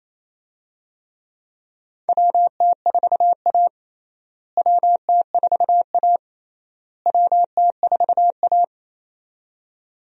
Morse code, heard as a keyed tone, about 28 wpm, 700 Hz. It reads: WT4A WT4A WT4A
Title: USA Call Signs - 4 Characters - Repeated 3x - 28wpm